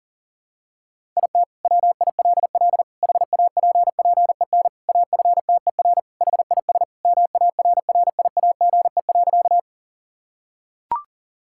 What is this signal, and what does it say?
Morse code, heard as a keyed tone, about 40 wpm, 700 Hz. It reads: IT WILL HAPPEN AFTER HIS MARRIAGE.
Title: Sentences from Top 1000 Words - Review Part 5 - 40wpm